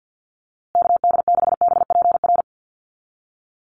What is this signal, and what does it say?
Morse code, heard as a keyed tone, about 50 wpm, 700 Hz. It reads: KD6BPR
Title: USA Call Signs Repeated 2x 50wpm